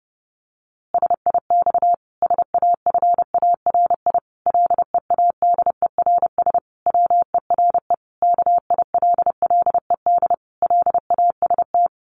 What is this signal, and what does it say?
Morse code, heard as a keyed tone, about 30 wpm, 700 Hz. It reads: SIX SAFARI LEADERS WERE KILLED LAST
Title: A Sound Of Thunder 30wpm